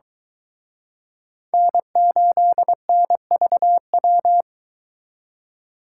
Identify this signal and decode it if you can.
Morse code, heard as a keyed tone, about 23 wpm, 700 Hz. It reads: N8NVW